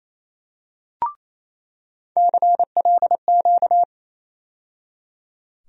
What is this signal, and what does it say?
Morse code, heard as a keyed tone, about 28 wpm, 700 Hz. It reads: CLQ
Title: USA Call Sign Postfixes - 3 Letters 28wpm